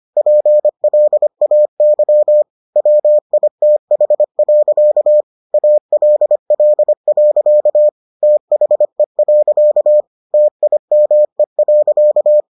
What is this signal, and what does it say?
Morse code, heard as a keyed tone, about 25 wpm, 600 Hz. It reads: PLAY WITH. ALL. THE. TIME.